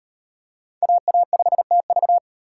Morse code, heard as a keyed tone, about 38 wpm, 700 Hz. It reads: AA5TV